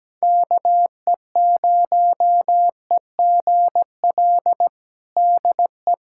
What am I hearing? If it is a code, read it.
Morse code, heard as a keyed tone, about 17 wpm, 700 Hz. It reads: KE0EGL DE
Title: USA Callsign DE Callsign 17wpm